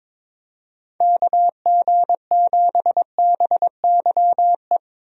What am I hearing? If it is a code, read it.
Morse code, heard as a keyed tone, about 22 wpm, 700 Hz. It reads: KG7BYE
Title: USA Call Signs - 6 Characters - Repeated 2x - 22wpm